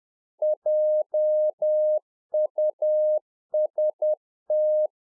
Morse code, heard as a keyed tone, about 10 wpm, 610 Hz. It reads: JUST